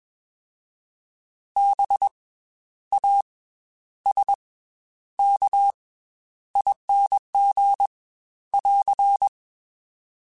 Morse code, heard as a keyed tone, about 21 wpm, 800 Hz. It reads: B A S K ING <AR>
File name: Special INGs / Special INGs-mp3s / Slow ing @20